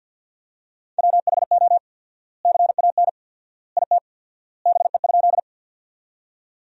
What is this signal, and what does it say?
Morse code, heard as a keyed tone, about 50 wpm, 700 Hz. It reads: WHO CAN IT BE?
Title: 4x Word-Spacing - Sentences from Top 100 Words - 50wpm